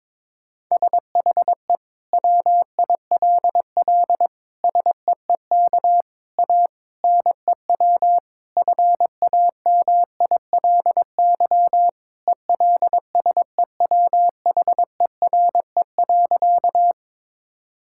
Morse code, heard as a keyed tone, about 22 wpm, 700 Hz. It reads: SHE WILL SEEK A NEW FAMILY ELSEWHERE.
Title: Sentences from Top 1600 Words 22wpm